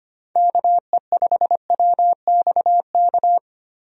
Morse code, heard as a keyed tone, about 25 wpm, 700 Hz. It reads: KE5WXK
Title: USA Call Signs - 6 Characters - 25wpm